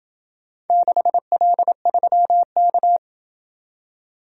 Morse code, heard as a keyed tone, about 25 wpm, 700 Hz. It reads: 6L3K
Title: Speed-Racing - Four Letter-Number Combinations - 20wpm